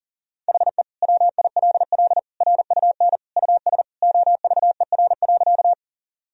Morse code, heard as a keyed tone, about 37 wpm, 700 Hz. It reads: HE WILL RUN US OVER.
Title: Sentences from Top 1000 Words - Review Part 1 - 40wpm